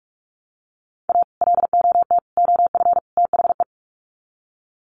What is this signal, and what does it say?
Morse code, heard as a keyed tone, about 44 wpm, 700 Hz. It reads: A LOT OF THE
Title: Sets of 4 Words 45wpm